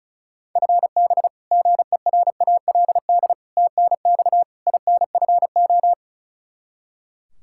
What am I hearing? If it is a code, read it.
Morse code, heard as a keyed tone, about 34 wpm, 700 Hz. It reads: FB GERALD TNX INFO
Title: Practice QSO - Exchange 3 - More Info - 35wpm